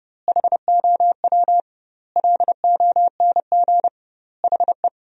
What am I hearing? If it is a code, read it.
Morse code, heard as a keyed tone, about 32 wpm, 700 Hz. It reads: HOW LONG HE
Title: 2x Word-Spacing - Sentences from Top 300 Words - 30wpm